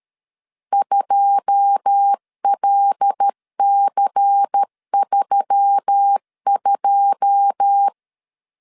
Morse code, heard as a keyed tone, about 13 wpm, 800 Hz. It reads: 2LC32
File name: LEZIONI / 9 2 12wpm 2301